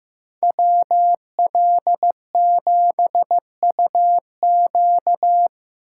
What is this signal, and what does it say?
Morse code, heard as a keyed tone, about 18 wpm, 700 Hz. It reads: WL7UQ
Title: Speed Racing - USA 2x2 Call Signs 15wpm